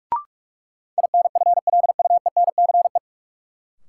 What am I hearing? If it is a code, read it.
Morse code, heard as a keyed tone, about 46 wpm, 700 Hz. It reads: INFLUENCE
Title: Sentences from Top 1100 Words - New Word Drill Alt - 45wpm